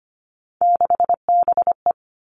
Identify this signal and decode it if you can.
Morse code, heard as a keyed tone, about 25 wpm, 700 Hz. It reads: 6BE
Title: Three Challenging Letter-Number Combinations 25wpm